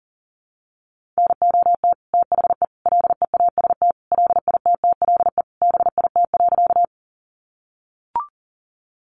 Morse code, heard as a keyed tone, about 40 wpm, 700 Hz. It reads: NOT THE LEAST LITTLE BIT.
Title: Sentences from Top 900 Words 40wpm